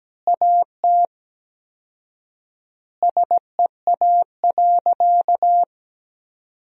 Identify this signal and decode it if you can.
Morse code, heard as a keyed tone, about 17 wpm, 700 Hz. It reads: AT SEA.
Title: 4x Word-Spacing - Sentences from Top 300 Words - 17wpm